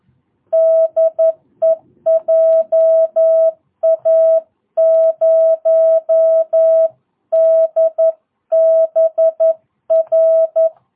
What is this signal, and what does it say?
Morse code, heard as a keyed tone, about 11 wpm, 650 Hz. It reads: DE JA0DBR
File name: jh8cbh / 2022cw